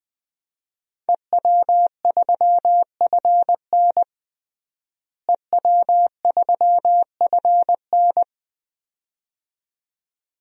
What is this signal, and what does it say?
Morse code, heard as a keyed tone, about 20 wpm, 700 Hz. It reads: EW3FN EW3FN
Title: DXCC-100 Call Signs - 5 Characters - Repeated 2x - 20wpm